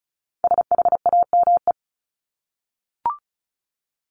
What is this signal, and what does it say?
Morse code, heard as a keyed tone, about 34 wpm, 700 Hz. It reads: SHAME
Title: Top 1751-2000 Words 35wpm